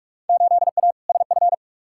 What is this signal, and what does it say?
Morse code, heard as a keyed tone, about 46 wpm, 700 Hz. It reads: 8A SF
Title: Field Day Exchanges Encore - 45wpm